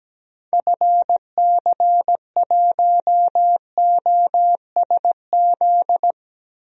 Morse code, heard as a keyed tone, about 17 wpm, 700 Hz. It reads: FC1OSZ
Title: DXCC-100 Call Signs - 6 Characters - Repeated 3x - 17wpm